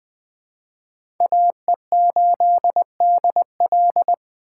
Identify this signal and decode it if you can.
Morse code, heard as a keyed tone, about 20 wpm, 700 Hz. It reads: AE8DL
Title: USA Call Signs - 4x Character Spacing - 20wpm